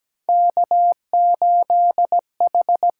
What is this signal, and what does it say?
Morse code, heard as a keyed tone, about 17 wpm, 700 Hz. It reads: K8H
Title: USA Call Signs - 4 Characters - 8x Character Spacing - 17wpm